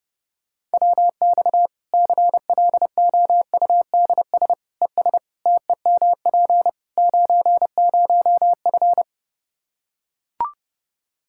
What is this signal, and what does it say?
Morse code, heard as a keyed tone, about 30 wpm, 700 Hz. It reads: WX CLOUDS ES TEMP 90F
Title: Practice QSO - Exchange 2 - Info - 30wpm